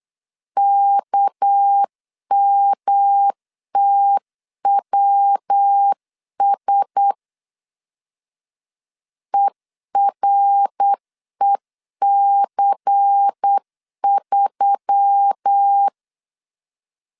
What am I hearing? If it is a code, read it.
Morse code, heard as a keyed tone, about 8 wpm, 800 Hz. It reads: KMTWS EREC3
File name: CW / DET / 1 6 1500C S C 8wpm 5219